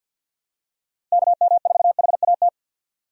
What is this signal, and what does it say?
Morse code, heard as a keyed tone, about 50 wpm, 700 Hz. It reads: KM4HAT
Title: USA Call Signs - 6 Characters - Repeated 2x - 50wpm